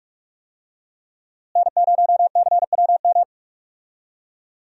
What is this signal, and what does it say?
Morse code, heard as a keyed tone, about 46 wpm, 700 Hz. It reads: N0CWM
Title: USA Call Signs - 5 Characters - Repeated 3x - 45wpm